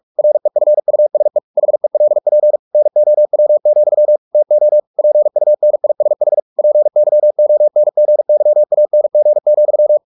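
Morse code, heard as a keyed tone, about 44 wpm, 600 Hz. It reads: REFUSE HELP NOW, TO PUNISH PYONGYANG,
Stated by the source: Morse practice at 45 wpm, the novel Never by Ken Follett in English, from CW med Gnister.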